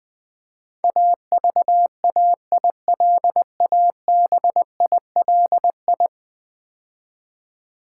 Morse code, heard as a keyed tone, about 20 wpm, 700 Hz. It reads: AVAILABILI
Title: Build A Word - 5 Letters At A Time - 12-Letter Words - 20wpm